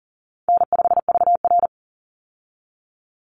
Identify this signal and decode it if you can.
Morse code, heard as a keyed tone, about 38 wpm, 700 Hz. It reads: N5VR
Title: Easy Call Signs - 40wpm